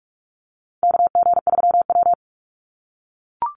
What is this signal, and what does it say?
Morse code, heard as a keyed tone, about 44 wpm, 700 Hz. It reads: KG3W
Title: NA SOTA Chaser Callsigns 45wpm